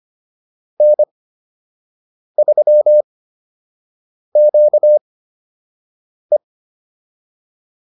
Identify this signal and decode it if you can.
Morse code, heard as a keyed tone, about 25 wpm, spaced as for 6 wpm, 600 Hz. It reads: N3QE